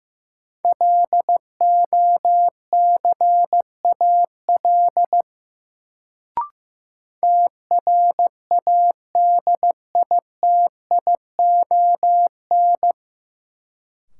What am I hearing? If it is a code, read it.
Morse code, heard as a keyed tone, about 15 wpm, 700 Hz. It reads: LOCAL TRADITION
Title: Sentences from Top 1500 Words - New Word Drill Alt - 15wpm